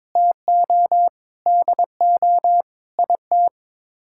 Morse code, heard as a keyed tone, about 22 wpm, 700 Hz. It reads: TO DO IT
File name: practice / 22wpm / Sets of 8 Words - 22wpm